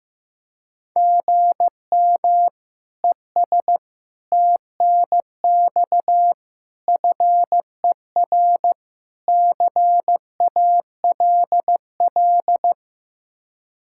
Morse code, heard as a keyed tone, about 15 wpm, 700 Hz. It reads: GM ES TNX FER CALL
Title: K1USN SST QSO Practice 15wpm